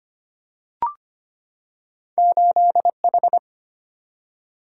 Morse code, heard as a keyed tone, about 25 wpm, 700 Hz. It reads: 8H
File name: Two Letter-Number Combinations - 25wpm